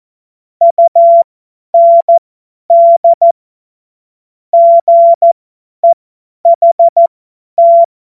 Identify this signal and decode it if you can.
Morse code, heard as a keyed tone, about 14 wpm, spaced as for 10 wpm, 680 Hz. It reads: UND GEHT
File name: Morsekurs / Witze / Witz 02. Dezember 2025 50 F70